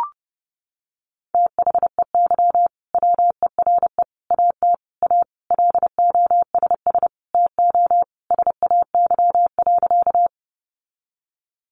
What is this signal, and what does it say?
Morse code, heard as a keyed tone, about 30 wpm, 700 Hz. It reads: THEY WERE AT A LOSS TO SAY.
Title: Sentences from Top 1400 Words 30wpm